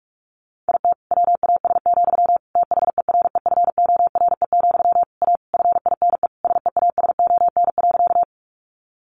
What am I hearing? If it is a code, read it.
Morse code, heard as a keyed tone, about 44 wpm, 700 Hz. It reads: IT WAS, THEREFORE, A FINE SEASON.
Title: Sentences from Top 1400 Words 45wpm